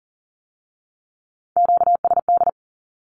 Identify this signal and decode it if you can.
Morse code, heard as a keyed tone, about 41 wpm, 700 Hz. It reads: QSD
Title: Q-Codes with Spoken Meaning - 40wpm